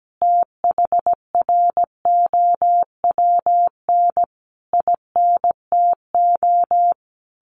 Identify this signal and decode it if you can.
Morse code, heard as a keyed tone, about 17 wpm, 700 Hz. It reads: THROWN INTO
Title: A Sound Of Thunder 17wpm